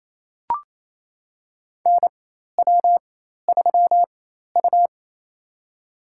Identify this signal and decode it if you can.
Morse code, heard as a keyed tone, about 28 wpm, spaced as for 13 wpm, 700 Hz. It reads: NW3U
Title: USA Call Signs - 4 Characters - 4x Character Spacing - 28wpm